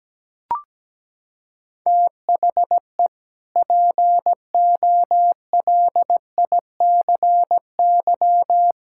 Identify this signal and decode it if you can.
Morse code, heard as a keyed tone, about 17 wpm, 700 Hz. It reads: THE POLICY
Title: Sentences from Top 1000 Words - Review Part 5 - 17wpm